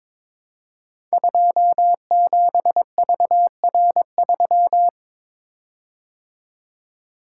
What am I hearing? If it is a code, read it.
Morse code, heard as a keyed tone, about 22 wpm, 700 Hz. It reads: 27VR3
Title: Code Groups - 5 Letter:Numbers - 22wpm